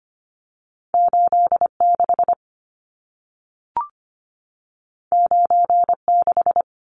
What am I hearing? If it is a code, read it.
Morse code, heard as a keyed tone, about 25 wpm, 700 Hz. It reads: 86 96
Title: Numbers - 2-Digit 25wpm